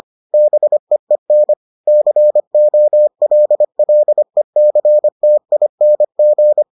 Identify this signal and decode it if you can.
Morse code, heard as a keyed tone, about 25 wpm, 600 Hz. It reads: BEEN COLLECTING